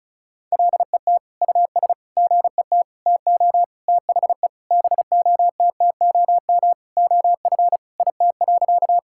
Morse code, heard as a keyed tone, about 34 wpm, 700 Hz. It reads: LET US GET TO THE BOTTOM OF IT.
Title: Sentences from Top 1000 Words - Review Part 6 - 35wpm